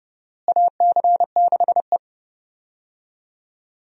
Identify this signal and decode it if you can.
Morse code, heard as a keyed tone, about 30 wpm, 700 Hz. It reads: AC6E
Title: USA Call Signs - 4 Characters - Repeated 2x - 30wpm